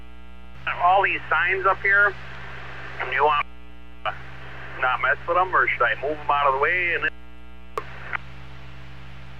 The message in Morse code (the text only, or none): none